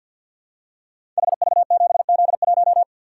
Morse code, heard as a keyed tone, about 50 wpm, 700 Hz. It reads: HV7Z1
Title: Code Groups - 5 Letter:Numbers - 50wpm